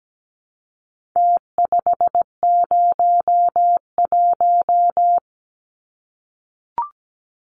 Morse code, heard as a keyed tone, about 17 wpm, 700 Hz. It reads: T501
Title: Numbers - Contest Style - 4-Digit 17wpm